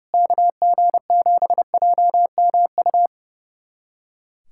Morse code, heard as a keyed tone, about 32 wpm, 700 Hz. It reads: KG7JMU
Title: USA Callsign DE Callsign - 6 Characters - 30wpm